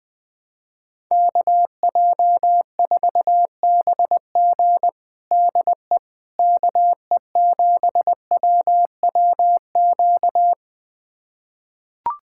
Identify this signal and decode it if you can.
Morse code, heard as a keyed tone, about 21 wpm, 700 Hz. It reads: KJ4BG DE KE7WWQ
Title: USA Callsign DE Callsign 20wpm